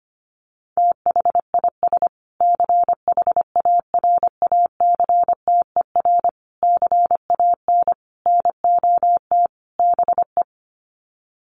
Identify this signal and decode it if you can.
Morse code, heard as a keyed tone, about 25 wpm, 700 Hz. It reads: THIS CHARACTER CAN NOT BE
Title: Sets of 5 Words Encore2 25wpm